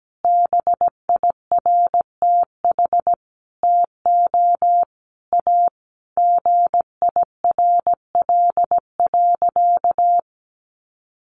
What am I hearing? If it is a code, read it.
Morse code, heard as a keyed tone, about 17 wpm, 700 Hz. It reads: BIRTH TO A GIRL.
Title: Sentences from Top 1100 Words 17wpm